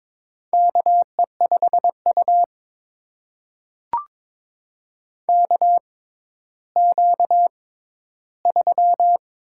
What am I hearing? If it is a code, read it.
Morse code, heard as a keyed tone, about 22 wpm, spaced as for 12 wpm, 700 Hz. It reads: KE5U KQ3
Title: USA Call Signs - 4 Characters - 6x Character Spacing - 22wpm